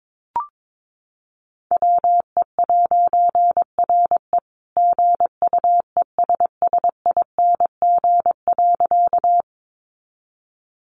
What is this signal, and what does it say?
Morse code, heard as a keyed tone, about 22 wpm, 700 Hz. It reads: WE'RE GUESSING.
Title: A Sound Of Thunder 22wpm